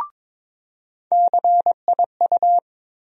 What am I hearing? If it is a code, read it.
Morse code, heard as a keyed tone, about 22 wpm, 700 Hz. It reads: CIU
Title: USA Call Sign Postfixes - 3 Letters 22wpm